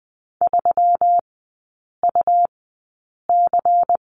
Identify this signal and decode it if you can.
Morse code, heard as a keyed tone, about 20 wpm, spaced as for 8 wpm, 700 Hz. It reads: 3UC
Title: 205 Lesson 33e - Letter j - Three Characters 20wpm